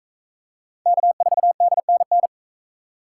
Morse code, heard as a keyed tone, about 42 wpm, 700 Hz. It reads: K4DNN